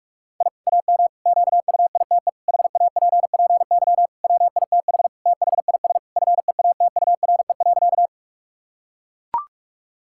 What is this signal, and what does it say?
Morse code, heard as a keyed tone, about 41 wpm, 700 Hz. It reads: I AM QUITE HAPPY WITH THIS FEATURE.
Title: Sentences from Top 1000 Words - Review Part 5 - 45wpm